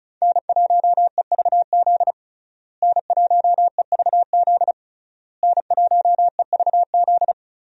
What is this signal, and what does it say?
Morse code, heard as a keyed tone, about 36 wpm, 700 Hz. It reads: N1EVZ N1EVZ N1EVZ
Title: USA Call Signs - 5 Characters - Repeated 3x - 35wpm